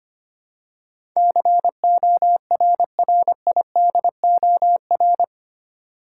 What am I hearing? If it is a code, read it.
Morse code, heard as a keyed tone, about 25 wpm, 700 Hz. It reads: CORRIDOR
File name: 8 Letter Words - First 3 Letters Given - 25wpm